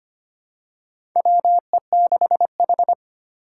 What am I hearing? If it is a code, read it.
Morse code, heard as a keyed tone, about 25 wpm, 700 Hz. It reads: WE6H
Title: USA Call Signs - 3x Character Spacing - 25wpm